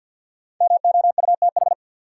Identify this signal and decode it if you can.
Morse code, heard as a keyed tone, about 52 wpm, 700 Hz. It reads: MOUTH
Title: Sentences from Top 1500 Words - New Word Drill Alt - 50wpm